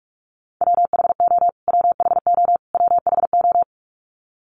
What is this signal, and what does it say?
Morse code, heard as a keyed tone, about 46 wpm, 700 Hz. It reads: WHO WHO WHO
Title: Top 300 Words Repeated 45wpm